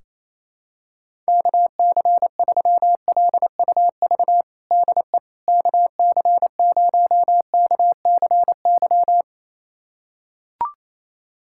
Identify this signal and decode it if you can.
Morse code, heard as a keyed tone, about 27 wpm, 700 Hz. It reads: KC3LUV DE KC0KCY
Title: USA Callsign DE Callsign 28wpm